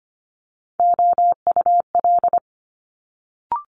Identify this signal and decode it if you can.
Morse code, heard as a keyed tone, about 25 wpm, 700 Hz. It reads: OUL
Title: Multi-letter Phonemes Quick Fire 25wpm